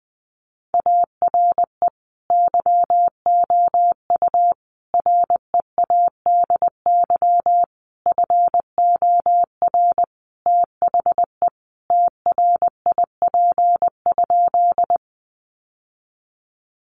Morse code, heard as a keyed tone, about 20 wpm, 700 Hz. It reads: ARE YOU READY FOR THE TRIP?
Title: Sentences from Top 1300 Words 20wpm